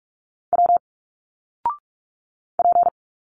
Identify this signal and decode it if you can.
Morse code, heard as a keyed tone, about 46 wpm, 700 Hz. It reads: W P
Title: Single Letters 45wpm